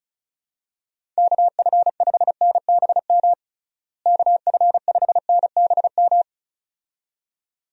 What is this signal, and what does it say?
Morse code, heard as a keyed tone, about 36 wpm, 700 Hz. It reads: KF5NBM KF5NBM
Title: USA Call Signs - 6 Characters - Repeated 2x - 35wpm